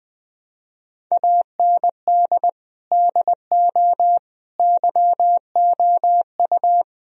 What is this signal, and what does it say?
Morse code, heard as a keyed tone, about 20 wpm, 700 Hz. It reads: AND DO YOU
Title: Sentences from Top 1000 Words - Review Part 3 - 20wpm